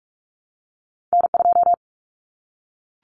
Morse code, heard as a keyed tone, about 46 wpm, 700 Hz. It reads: N2
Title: Common Cut Numbers - 2-Digit 45wpm